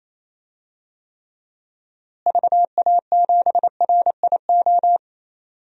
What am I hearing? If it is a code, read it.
Morse code, heard as a keyed tone, about 28 wpm, 700 Hz. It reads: VA7RIO
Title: DXCC-100 Call Signs - 6 Characters - Repeated 2x - 28wpm